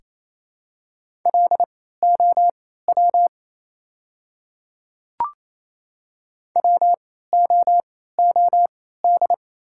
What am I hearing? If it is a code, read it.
Morse code, heard as a keyed tone, about 27 wpm, spaced as for 16 wpm, 700 Hz. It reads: LOW WOOD
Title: 3x Character Spacing - Top 300 Words - 28wpm